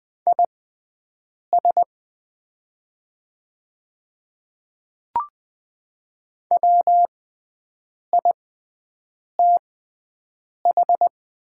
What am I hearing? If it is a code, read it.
Morse code, heard as a keyed tone, about 20 wpm, spaced as for 7 wpm, 700 Hz. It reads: IS WITH
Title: 6x Character Spacing - Top 100 Words - 20wpm